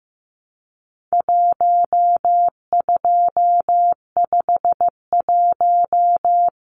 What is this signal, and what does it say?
Morse code, heard as a keyed tone, about 15 wpm, 700 Hz. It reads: 1251
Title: Numbers - Contest Style - 4-Digit 15wpm